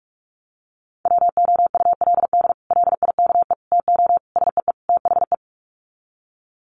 Morse code, heard as a keyed tone, about 44 wpm, 700 Hz. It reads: WOULD LIKE TO SEE THE
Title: Sets of 5 Words Encore2 45wpm